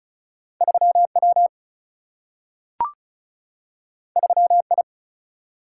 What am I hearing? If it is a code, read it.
Morse code, heard as a keyed tone, about 36 wpm, 700 Hz. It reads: 3 W 3 I
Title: Single-Letter Single-Number 35wpm